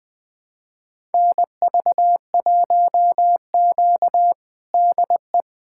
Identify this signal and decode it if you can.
Morse code, heard as a keyed tone, about 20 wpm, 700 Hz. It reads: NV1Q DE
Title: USA Callsign DE Callsign - 4 Characters - 20wpm